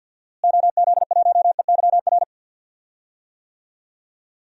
Morse code, heard as a keyed tone, about 49 wpm, 700 Hz. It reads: OZ1EYR
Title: DXCC-100 Call Signs - 6 Characters - Repeated 2x - 50wpm